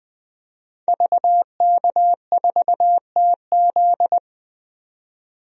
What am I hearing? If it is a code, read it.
Morse code, heard as a keyed tone, about 20 wpm, 700 Hz. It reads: VK4TZ
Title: DXCC-100 Call Signs - 5 Characters - 20wpm